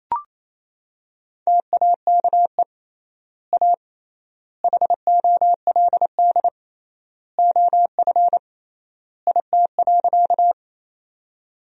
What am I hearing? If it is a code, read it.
Morse code, heard as a keyed tone, about 28 wpm, 700 Hz. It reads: TAKE A HOLD OF IT.
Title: 3x Word-Spacing - Sentences from Top 400 Words - 28wpm